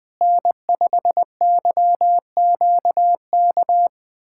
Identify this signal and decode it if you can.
Morse code, heard as a keyed tone, about 21 wpm, 700 Hz. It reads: N5YQK
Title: USA Callsign DE Callsign - 5 Characters - 20wpm